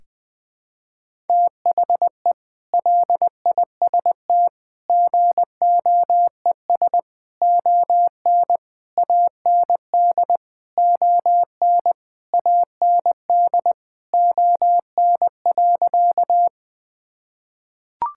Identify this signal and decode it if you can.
Morse code, heard as a keyed tone, about 20 wpm, 700 Hz. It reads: THE LIST GOES ON AND ON AND ON.
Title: Sentences with 8-Word Phrases - 20wpm